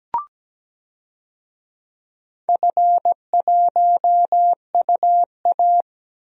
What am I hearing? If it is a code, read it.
Morse code, heard as a keyed tone, about 17 wpm, 700 Hz. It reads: F1UA
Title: DXCC-100 Call Signs - 4 Characters - 17wpm